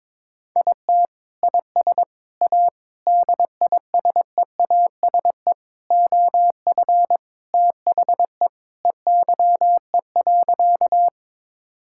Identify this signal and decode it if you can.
Morse code, heard as a keyed tone, about 22 wpm, 700 Hz. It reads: IT IS A DISEASE OF THE EYE.
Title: Sentences from Top 1000 Words - Review Part 6 - 22wpm